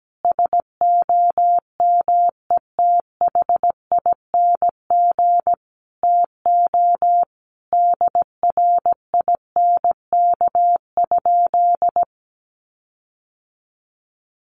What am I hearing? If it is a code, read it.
Morse code, heard as a keyed tone, about 17 wpm, 700 Hz. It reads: SOMETHING TO DRINK?
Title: Sentences from Top 1500 Words 17wpm